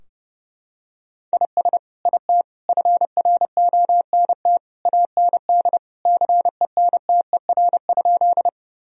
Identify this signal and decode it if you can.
Morse code, heard as a keyed tone, about 30 wpm, 700 Hz. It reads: IS IT FRONT AND CENTER?
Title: Sentences from Top 1000 Words - Review Part 2 - 30wpm